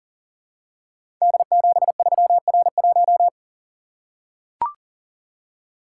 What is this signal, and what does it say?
Morse code, heard as a keyed tone, about 38 wpm, 700 Hz. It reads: D73R1
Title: Code Groups - 5 Letter:Numbers - 40wpm